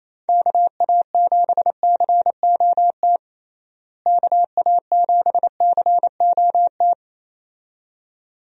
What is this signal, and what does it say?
Morse code, heard as a keyed tone, about 28 wpm, 700 Hz. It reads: KA7COT KA7COT
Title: USA Call Signs - 6 Characters - Repeated 2x - 28wpm